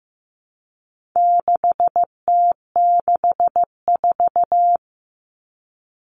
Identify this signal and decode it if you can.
Morse code, heard as a keyed tone, about 15 wpm, 700 Hz. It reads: 6T64
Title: All Cut Numbers - 4-Digit 15wpm